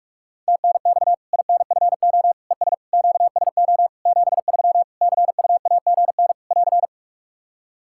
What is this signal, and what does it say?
Morse code, heard as a keyed tone, about 46 wpm, 700 Hz. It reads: TNX INFO ES QSO 73 CUAGN <AR>
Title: Practice QSO - Exchange 4 - Ending - 45wpm